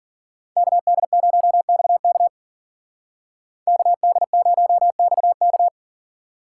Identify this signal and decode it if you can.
Morse code, heard as a keyed tone, about 51 wpm, 700 Hz. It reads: KD0XK KD0XK